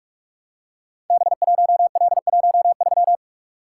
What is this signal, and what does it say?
Morse code, heard as a keyed tone, about 44 wpm, 700 Hz. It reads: B1L13
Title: Code Groups - 5 Letter:Numbers - 45wpm